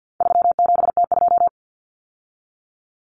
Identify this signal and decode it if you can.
Morse code, heard as a keyed tone, about 50 wpm, 700 Hz. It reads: 37T2
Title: Common Cut Numbers - 4-Digit 50wpm